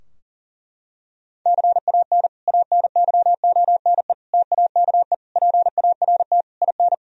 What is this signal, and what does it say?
Morse code, heard as a keyed tone, about 40 wpm, 700 Hz. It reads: CAN ANYONE TAKE PART IN
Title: Sentences from Top 1000 Words - Review Part 4 - 40wpm